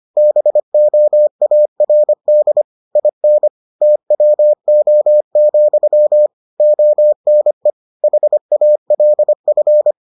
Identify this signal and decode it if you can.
Morse code, heard as a keyed tone, about 25 wpm, 600 Hz. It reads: BOARD IN TWO, ONE HALF